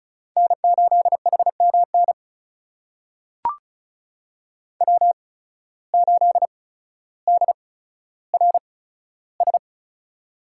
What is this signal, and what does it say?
Morse code, heard as a keyed tone, about 36 wpm, spaced as for 15 wpm, 690 Hz. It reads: N8HMN W8DRS
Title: USA Call Signs - 5 Characters - 8x Character Spacing - 35wpm